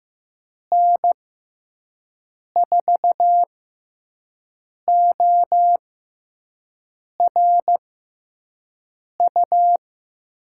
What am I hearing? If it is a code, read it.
Morse code, heard as a keyed tone, about 15 wpm, spaced as for 5 wpm, 700 Hz. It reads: N4ORU
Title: USA Call Signs - 6 Characters - 6x Character Spacing - 15wpm